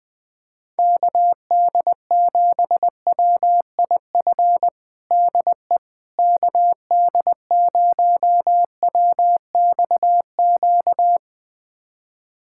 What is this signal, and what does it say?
Morse code, heard as a keyed tone, about 20 wpm, 700 Hz. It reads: KD7WIF DE KD0WXQ
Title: USA Callsign DE Callsign - 6 Characters - 20wpm